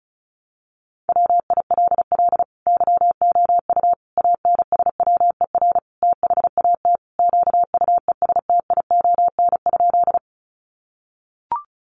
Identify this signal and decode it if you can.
Morse code, heard as a keyed tone, about 34 wpm, 700 Hz. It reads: WILL YOU ANSWER THAT QUESTION?
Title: Sentences from Top 400 Words 35wpm